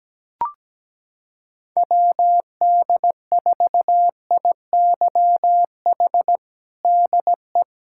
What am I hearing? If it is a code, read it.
Morse code, heard as a keyed tone, about 17 wpm, 700 Hz. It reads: WD4IYH DE
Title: Practice QSO - Exchange 4 - Ending - 17wpm